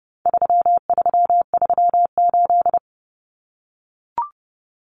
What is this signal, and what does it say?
Morse code, heard as a keyed tone, about 30 wpm, 700 Hz. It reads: 3338
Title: Common Cut Numbers - 4-Digit 30wpm